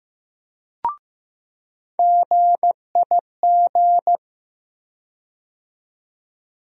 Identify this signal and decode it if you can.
Morse code, heard as a keyed tone, about 15 wpm, 700 Hz. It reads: GIG
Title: USA Call Sign Postfixes - 3 Letters 15wpm